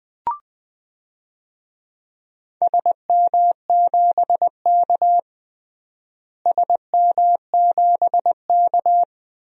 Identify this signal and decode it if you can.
Morse code, heard as a keyed tone, about 20 wpm, 700 Hz. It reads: SM7K SM7K
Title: DXCC-100 Call Signs - 4 Characters - Repeated 2x - 20wpm